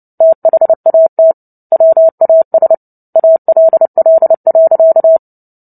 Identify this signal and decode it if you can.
Morse code, heard as a keyed tone, about 30 wpm, 650 Hz. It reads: THAT WAS ALL.
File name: Book Two / 03-Chapter 3 - 29 WPM